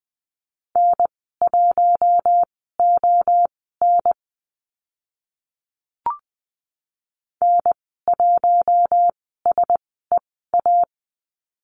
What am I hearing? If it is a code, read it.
Morse code, heard as a keyed tone, about 20 wpm, spaced as for 14 wpm, 700 Hz. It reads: N1ON N1SEA
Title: 042 Lesson 8g - Number 1 - Call Signs 20wpm